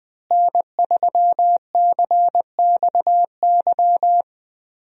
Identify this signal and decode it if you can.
Morse code, heard as a keyed tone, about 20 wpm, 700 Hz. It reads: N3CXY
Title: USA Call Signs - 5 Characters - Repeated 3x - 20wpm